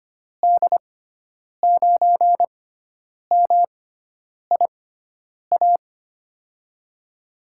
Morse code, heard as a keyed tone, about 25 wpm, spaced as for 9 wpm, 700 Hz. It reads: D9MIA
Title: USA Call Signs - 6 Characters - 6x Character Spacing - 25wpm